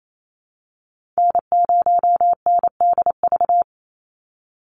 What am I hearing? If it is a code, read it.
Morse code, heard as a keyed tone, about 28 wpm, 700 Hz. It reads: N0NDV
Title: USA Call Signs 28wpm